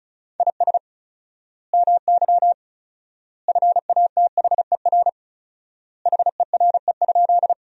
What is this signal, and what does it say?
Morse code, heard as a keyed tone, about 36 wpm, 700 Hz. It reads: IS MY FATHER HERE?
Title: 4x Word-Spacing - Sentences from Top 300 Words - 35wpm